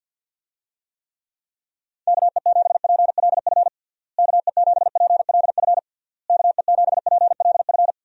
Morse code, heard as a keyed tone, about 53 wpm, 700 Hz. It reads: KE7PLF KE7PLF KE7PLF
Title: USA Call Signs - 6 Characters - Repeated 3x - 50wpm